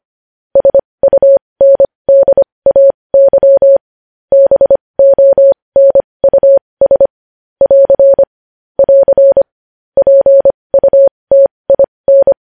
Morse code, heard as a keyed tone, about 25 wpm, spaced as for 20 wpm, 550 Hz. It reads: SUNDAY BONUS <AR> <AR> PUTIN
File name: podcast / News headlines-1x-25x20